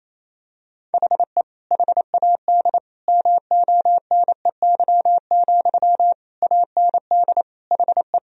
Morse code, heard as a keyed tone, about 28 wpm, 700 Hz. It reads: HE HAD MONEY, AND HE
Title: Sentences from Top 1000 Words - Review Part 3 - 28wpm